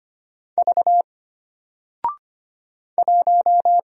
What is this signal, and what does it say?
Morse code, heard as a keyed tone, about 25 wpm, 700 Hz. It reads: V 1
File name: Single-Letter-Number-25wpm